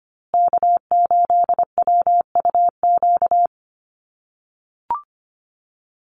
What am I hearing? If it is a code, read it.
Morse code, heard as a keyed tone, about 25 wpm, 700 Hz. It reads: K8WUQ
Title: USA Call Signs Repeated 3x 25wpm